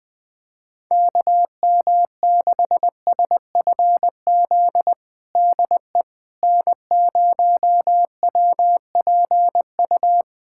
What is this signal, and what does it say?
Morse code, heard as a keyed tone, about 20 wpm, 700 Hz. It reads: KM6SFZ DE N0WPU
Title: USA Callsign DE Callsign 20wpm